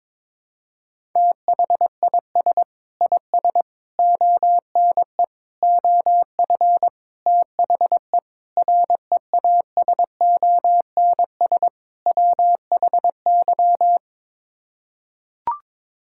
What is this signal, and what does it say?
Morse code, heard as a keyed tone, about 21 wpm, 700 Hz. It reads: THIS IS ONE OF THE REASONS WHY
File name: practice / 22wpm / Sets of 7 Words - 22wpm